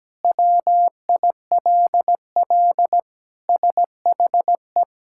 Morse code, heard as a keyed tone, about 17 wpm, 700 Hz. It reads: WILL SHE
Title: Sentences from Top 1000 Words - Review Part 6 - 17wpm